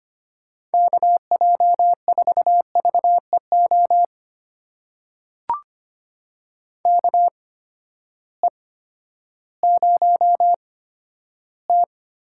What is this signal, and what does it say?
Morse code, heard as a keyed tone, about 25 wpm, spaced as for 12 wpm, 700 Hz. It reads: KJ4VEO KE0T
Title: USA Call Signs - 6 Characters - 8x Character Spacing - 25wpm